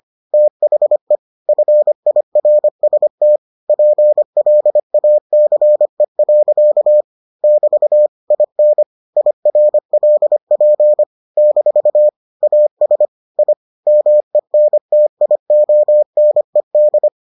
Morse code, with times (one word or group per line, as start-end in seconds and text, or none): 0.3-1.2 THE
1.5-3.4 FIRST
3.7-7.0 PLACE.
7.4-8.1 <BT>
8.3-8.8 IN
9.2-11.0 IRLP
11.4-12.1 -
12.4-13.1 AS
13.4-13.5 I
13.9-17.1 MENTIONED